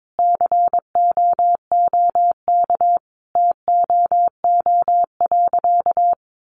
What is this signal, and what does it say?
Morse code, heard as a keyed tone, about 22 wpm, 700 Hz. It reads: COOK TOO.
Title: Sentences from Top 700 Words 22wpm